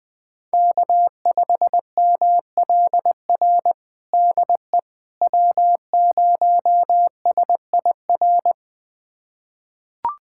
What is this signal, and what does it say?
Morse code, heard as a keyed tone, about 20 wpm, 700 Hz. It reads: K5MLR DE W0SIR
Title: USA Callsign DE Callsign - 5 Characters - 20wpm